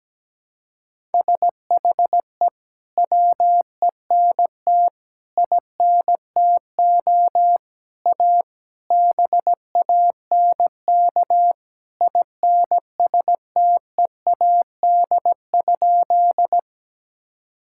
Morse code, heard as a keyed tone, about 17 wpm, 700 Hz. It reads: SHE WENT INTO A BANK INSTEAD?
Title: Sentences from Top 1000 Words - Review Part 6 - 17wpm